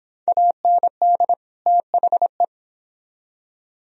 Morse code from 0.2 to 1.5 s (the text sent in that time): AND